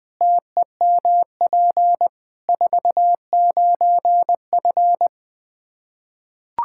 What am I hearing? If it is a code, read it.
Morse code, heard as a keyed tone, about 20 wpm, 700 Hz. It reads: TEMP 49F
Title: K1USN SST QSO Practice 20wpm